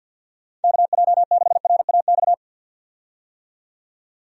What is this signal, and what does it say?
Morse code, heard as a keyed tone, about 52 wpm, 700 Hz. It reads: KJ6RAX
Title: USA Call Signs - 6 Characters - 50wpm